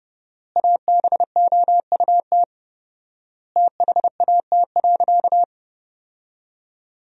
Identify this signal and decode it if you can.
Morse code, heard as a keyed tone, about 30 wpm, 700 Hz. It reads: ABOUT THAT.
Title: 4x Word-Spacing - Sentences from Top 200 Words - 30wpm